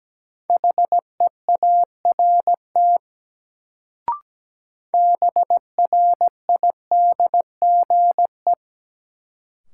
Morse code, heard as a keyed tone, about 17 wpm, 700 Hz. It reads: HEART BRIDGE
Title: Sentences from Top 1200 Words - New Word Drill - 17wpm